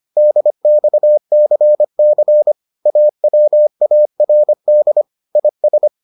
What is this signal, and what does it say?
Morse code, heard as a keyed tone, about 25 wpm, 600 Hz. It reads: DXCC AWARD IS